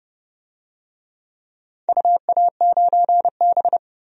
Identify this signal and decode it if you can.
Morse code, heard as a keyed tone, about 31 wpm, 700 Hz. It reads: UA9B
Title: DXCC-100 Call Signs - 4 Characters - 30wpm